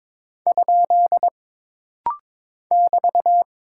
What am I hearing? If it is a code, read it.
Morse code, heard as a keyed tone, about 22 wpm, 700 Hz. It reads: ? <BT>
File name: practice / 22wpm / Single Punctuation-Prosign - Rapid-Fire - 22wpm